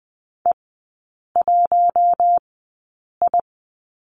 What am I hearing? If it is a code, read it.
Morse code, heard as a keyed tone, about 20 wpm, spaced as for 8 wpm, 700 Hz. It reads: E1I
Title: 041 Lesson 8e - Number 1 - Three Characters 20wpm